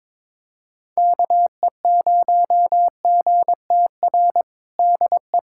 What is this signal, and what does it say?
Morse code, heard as a keyed tone, about 22 wpm, 700 Hz. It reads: KE0GTR DE
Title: USA Callsign DE Callsign - 6 Characters - 22wpm